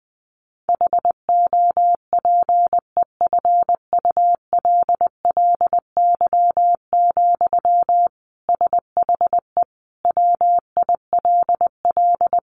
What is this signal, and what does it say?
Morse code, heard as a keyed tone, about 20 wpm, 700 Hz. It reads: HOPEFULLY, SHE WILL
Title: Sentences from Top 1500 Words 20wpm